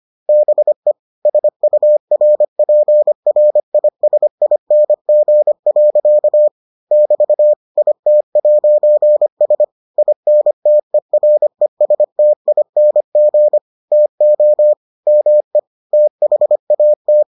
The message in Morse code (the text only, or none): BE SURPRISING. <BT> IT'S INTERESTING TO ME THAT